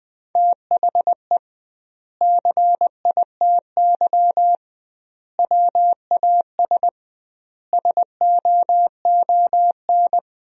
Morse code, heard as a keyed tone, about 20 wpm, 700 Hz. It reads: THE CITY WAS SOON
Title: 2x Word-Spacing - Sentences from Top 500 Words - 20wpm